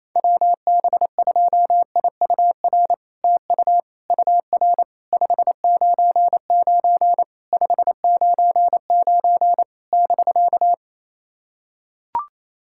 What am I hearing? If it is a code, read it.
Morse code, heard as a keyed tone, about 28 wpm, 700 Hz. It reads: WB2IUR TU UR 599 599 <BK>
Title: SOTA QSO Practice 28wpm